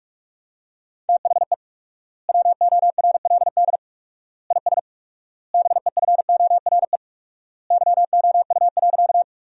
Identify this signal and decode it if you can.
Morse code, heard as a keyed tone, about 46 wpm, 700 Hz. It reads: THE WORLD IS BEFORE YOU.
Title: 4x Word-Spacing - Sentences from Top 200 Words - 45wpm